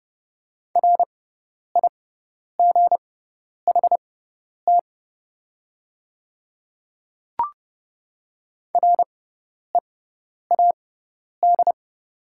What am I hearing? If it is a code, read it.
Morse code, heard as a keyed tone, about 30 wpm, spaced as for 10 wpm, 700 Hz. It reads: RIGHT READ
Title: 6x Character Spacing - Top 200 Words - 30wpm